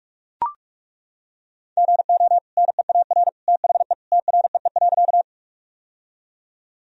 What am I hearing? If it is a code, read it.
Morse code, heard as a keyed tone, about 42 wpm, 700 Hz. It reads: GO NEAR THE TREE.
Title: Sentences from Top 1000 Words - Review Part 1 - 45wpm